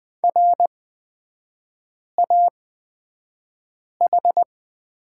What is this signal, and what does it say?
Morse code, heard as a keyed tone, about 20 wpm, spaced as for 5 wpm, 700 Hz. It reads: RAH
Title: 255 Lesson 41e - Period - Three Characters 20wpm